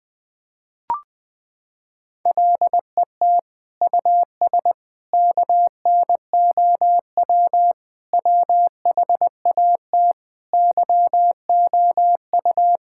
Morse code, heard as a keyed tone, about 20 wpm, 700 Hz. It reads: LET US KNOW WHAT YOU
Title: Sets of 8 Words - 20wpm